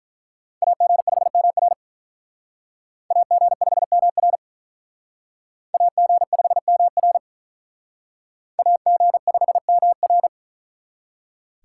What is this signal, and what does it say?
Morse code, heard as a keyed tone, about 44 wpm, 700 Hz. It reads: AG5MR AG5MR AG5MR AG5MR